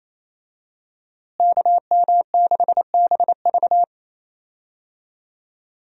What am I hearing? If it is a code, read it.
Morse code, heard as a keyed tone, about 28 wpm, 700 Hz. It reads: KM6BV
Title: USA Call Signs - 5 Characters - Repeated 2x - 28wpm